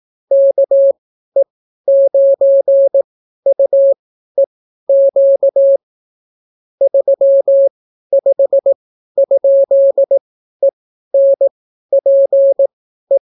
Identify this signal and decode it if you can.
Morse code, heard as a keyed tone, about 18 wpm, spaced as for 12 wpm, 550 Hz. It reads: KE9UEQ 35?ENPE